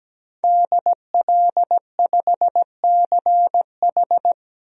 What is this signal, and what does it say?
Morse code, heard as a keyed tone, about 17 wpm, 700 Hz. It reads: DL5CH
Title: DXCC-100 Call Signs - 5 Characters - Repeated 3x - 17wpm